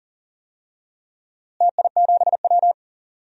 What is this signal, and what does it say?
Morse code, heard as a keyed tone, about 40 wpm, 700 Hz. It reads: TI7W
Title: DXCC-100 Call Signs - 4 Characters - Repeated 3x - 40wpm